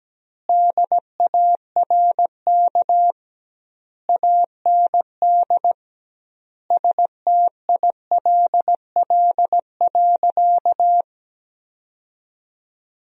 Morse code, heard as a keyed tone, about 17 wpm, 700 Hz. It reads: DARK AND STILL.
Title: 2x Word-Spacing - Sentences from Top 500 Words - 17wpm